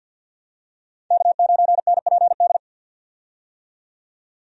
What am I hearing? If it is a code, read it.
Morse code, heard as a keyed tone, about 47 wpm, 700 Hz. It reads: K9NPD
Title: USA Call Signs - 5 Characters - 50wpm